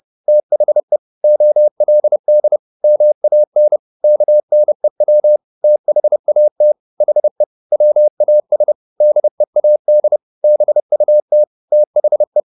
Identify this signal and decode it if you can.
Morse code, heard as a keyed tone, about 30 wpm, 600 Hz. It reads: THE OLD MAN KNEW THAT HE WAS DEAD BUT THE